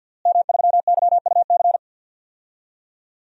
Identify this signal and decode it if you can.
Morse code, heard as a keyed tone, about 52 wpm, 700 Hz. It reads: M3YUC
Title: DXCC-100 Call Signs - 5 Characters - Repeated 3x - 50wpm